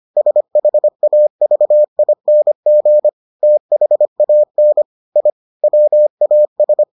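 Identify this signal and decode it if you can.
Morse code, heard as a keyed tone, about 25 wpm, 600 Hz. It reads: SHAVING THAN I WAS
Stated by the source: Morse edition of the Foundations of Amateur Radio episodes.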